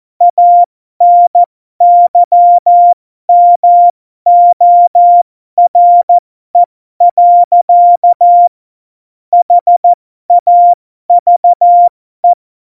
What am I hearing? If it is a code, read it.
Morse code, heard as a keyed tone, about 14 wpm, spaced as for 12 wpm, 700 Hz. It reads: ANYMORE. HAVE
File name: Morsekurs / Witze